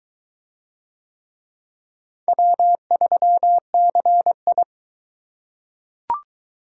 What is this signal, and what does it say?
Morse code, heard as a keyed tone, about 21 wpm, 700 Hz. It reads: W3CI